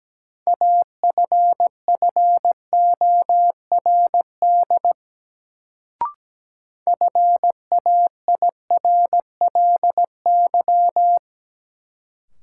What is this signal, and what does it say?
Morse code, heard as a keyed tone, about 17 wpm, 700 Hz. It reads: AFFORD FAIRLY
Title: Sentences from Top 1100 Words - New Word Drill Alt - 17wpm